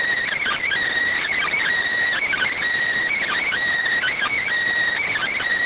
Emergency Alert System (SAME) tones (header bursts), heard as harsh buzzing bursts. It is heard from 0.3 to 4.5 s.